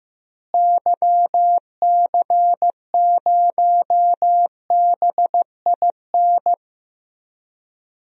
Morse code, heard as a keyed tone, about 15 wpm, 700 Hz. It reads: YC0BIN
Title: DXCC-100 Call Signs - 6 Characters - Repeated 3x - 15wpm